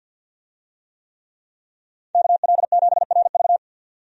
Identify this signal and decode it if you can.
Morse code, heard as a keyed tone, about 52 wpm, 700 Hz. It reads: KL7RV
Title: USA Call Signs - 5 Characters - 50wpm